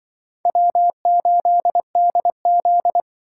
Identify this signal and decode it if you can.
Morse code, heard as a keyed tone, about 24 wpm, 700 Hz. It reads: W8DZ
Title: Speed Racing - USA 1x2 Call Signs 20wpm